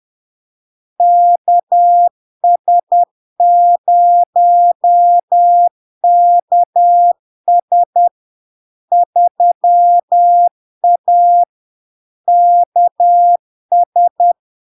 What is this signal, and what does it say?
Morse code, heard as a keyed tone, about 10 wpm, 700 Hz. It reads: KS0KS 3A KS
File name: LARK-FD-2022-Log-in-CW-10wpm-700-Hz